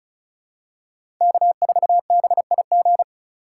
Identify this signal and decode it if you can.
Morse code, heard as a keyed tone, about 34 wpm, 700 Hz. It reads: K4BIG
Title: USA Call Signs - 5 Characters - 2x Character Spacing - 35wpm